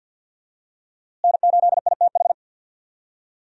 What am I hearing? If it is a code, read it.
Morse code, heard as a keyed tone, about 52 wpm, 700 Hz. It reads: N8ITH
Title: USA Call Signs - 5 Characters - Repeated 3x - 50wpm